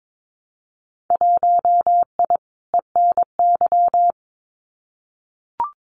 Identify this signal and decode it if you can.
Morse code, heard as a keyed tone, about 22 wpm, 700 Hz. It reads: 1I ENY
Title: Winter Field Day Exchanges Encore 22wpm